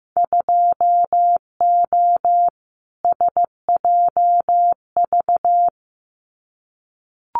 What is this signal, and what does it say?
Morse code, heard as a keyed tone, about 15 wpm, 700 Hz. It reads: 2O SJV
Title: Winter Field Day Exchanges Encore 15wpm